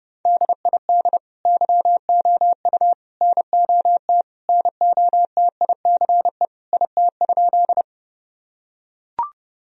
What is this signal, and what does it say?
Morse code, heard as a keyed tone, about 30 wpm, 700 Hz. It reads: DID YOU NOT NOTICE IT?
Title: Sentences from Top 1000 Words - Review Part 2 - 30wpm